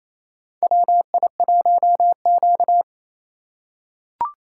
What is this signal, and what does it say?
Morse code, heard as a keyed tone, about 28 wpm, 700 Hz. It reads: WI1Q
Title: USA Call Signs - 4 Characters - 2x Character Spacing - 28wpm